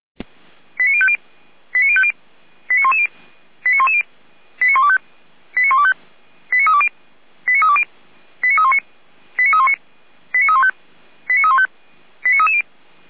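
Five-tone selective call (ZVEI), heard as quick runs of five tones, tones of about 70 ms. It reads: 89050 89050 89110 89110 89125 89125 89320 89320 89319 89319 89315 89315 89330
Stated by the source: ZVEI five-tone call received by SDR, fire brigade feuerwehr-orlamuende.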